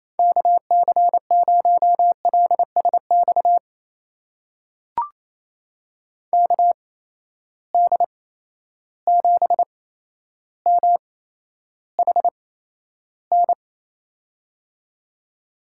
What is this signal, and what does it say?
Morse code, heard as a keyed tone, about 28 wpm, spaced as for 12 wpm, 700 Hz. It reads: KC0LSX KD7MHN
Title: USA Call Signs - 6 Characters - 8x Character Spacing - 28wpm